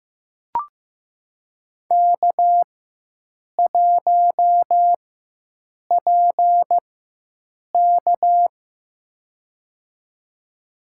Morse code, heard as a keyed tone, about 15 wpm, spaced as for 7 wpm, 700 Hz. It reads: K1PK
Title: USA Call Signs - 4 Characters - 4x Character Spacing - 15wpm